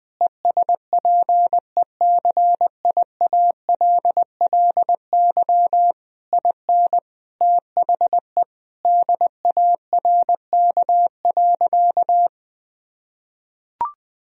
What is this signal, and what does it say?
Morse code, heard as a keyed tone, about 19 wpm, 700 Hz. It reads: ESPECIALLY IN THE DARK.
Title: Sentences from Top 1000 Words - Review Part 4 - 20wpm